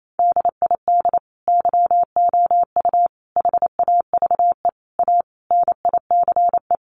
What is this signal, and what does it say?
Morse code, heard as a keyed tone, about 28 wpm, 700 Hz. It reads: DID YOU HAVE A NICE
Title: Sentences from Top 1100 Words 28wpm